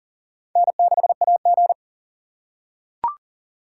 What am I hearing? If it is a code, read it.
Morse code, heard as a keyed tone, about 38 wpm, 700 Hz. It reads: N6AG
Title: USA Call Signs - 6x Character Spacing - 40wpm